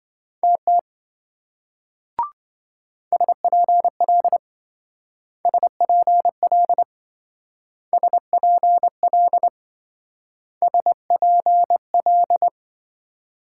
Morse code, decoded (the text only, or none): TT SPL SPL SPL SPL